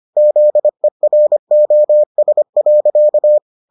Morse code, heard as a keyed tone, about 25 wpm, 600 Hz. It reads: ZEROS.